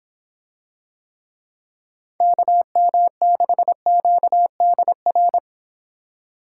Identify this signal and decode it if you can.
Morse code, heard as a keyed tone, about 26 wpm, 700 Hz. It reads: KM6QDR